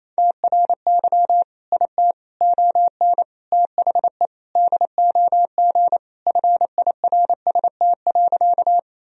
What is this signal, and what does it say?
Morse code, heard as a keyed tone, about 28 wpm, 700 Hz. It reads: TRY IT ON THE DOG FIRST.
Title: Sentences from Top 1000 Words - Review Part 2 - 28wpm